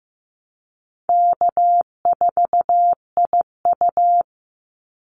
Morse code, heard as a keyed tone, about 15 wpm, 700 Hz. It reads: K4IU
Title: Easy Call Signs - 15wpm